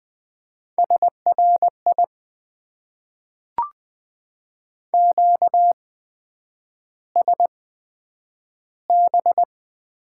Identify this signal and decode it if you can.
Morse code, heard as a keyed tone, about 20 wpm, spaced as for 9 wpm, 700 Hz. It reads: SRI QSB
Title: Top 90 QSO Elements - 8x Character Spacing - 20wpm